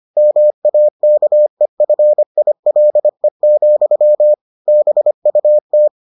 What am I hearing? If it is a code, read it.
Morse code, heard as a keyed tone, about 25 wpm, 600 Hz. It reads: MAKEFILE, BUT